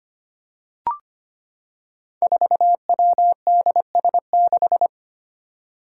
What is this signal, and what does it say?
Morse code, heard as a keyed tone, about 25 wpm, 700 Hz. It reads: 4WDS6
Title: Five Challenging Letter-Number Combinations 25wpm